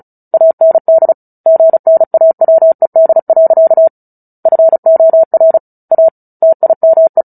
Morse code, heard as a keyed tone, about 34 wpm, 650 Hz. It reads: AND GNAWED. FOR A TIME
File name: Book Two / 09-Chapter 9 - 35 WPM